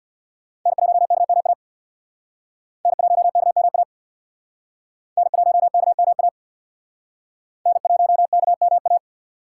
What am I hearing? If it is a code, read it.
Morse code, heard as a keyed tone, about 58 wpm, 700 Hz. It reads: N1KMA N1KMA N1KMA N1KMA
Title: Speed Racing - USA Call Signs 50wpm